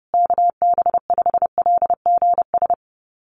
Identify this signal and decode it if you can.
Morse code, heard as a keyed tone, about 30 wpm, 700 Hz. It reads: KB5LGS
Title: USA Call Signs Repeated 3x 30wpm